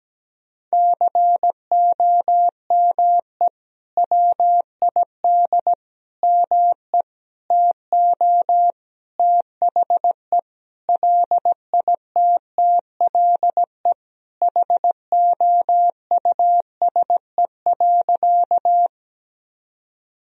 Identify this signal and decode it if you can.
Morse code, heard as a keyed tone, about 17 wpm, 700 Hz. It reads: COME WID ME TO THE LITTLE HOUSE.
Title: Sentences from Top 200 Words and Abbreviations 17wpm